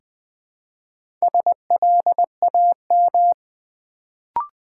Morse code, heard as a keyed tone, about 20 wpm, 700 Hz. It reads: SLAM
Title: Build A Word - 2 Letters At A Time - 4-Letter Words - 20wpm